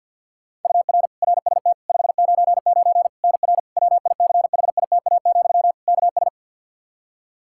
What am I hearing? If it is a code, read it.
Morse code, heard as a keyed tone, about 52 wpm, 700 Hz. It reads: UR RST 599 NR WICHITA, KS
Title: K1USN SST QSO Practice 50wpm